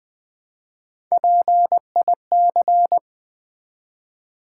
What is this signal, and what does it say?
Morse code, heard as a keyed tone, about 20 wpm, 700 Hz. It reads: PIC
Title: Build A Word - 1 Letter At A Time - 4-Letter Words - 20wpm